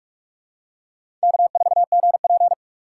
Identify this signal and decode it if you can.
Morse code, heard as a keyed tone, about 45 wpm, 700 Hz. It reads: K4GP
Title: USA Call Signs - 4 Characters - 3x Character Spacing - 45wpm